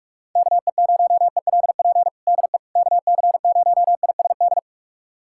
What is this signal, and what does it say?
Morse code, heard as a keyed tone, about 44 wpm, 700 Hz. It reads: KE0ELP DE KC0ISD
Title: USA Callsign DE Callsign - 6 Characters - 45wpm